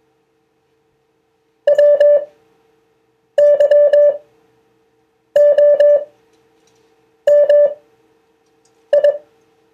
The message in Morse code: WYOMI